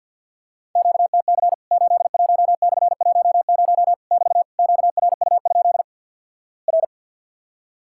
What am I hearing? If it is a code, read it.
Morse code, heard as a keyed tone, about 48 wpm, 700 Hz. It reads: QTC 81/10 <BT> QRU? R
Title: Worked All Europe DX Contest - QTC Exchange - 50wpm